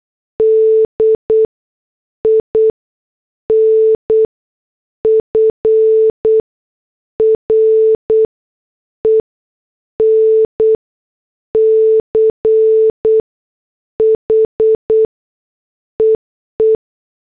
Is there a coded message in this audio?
Yes: Morse code from 0.4 to 16.8 s, heard as a keyed tone, about 8 wpm, spaced as for 6 wpm, 440 Hz. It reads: D IN FRENCH EE